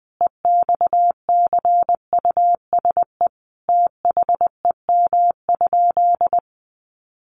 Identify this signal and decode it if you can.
Morse code, heard as a keyed tone, about 20 wpm, 700 Hz. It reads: EXCUSE THEM?
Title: Sentences from Top 700 Words 20wpm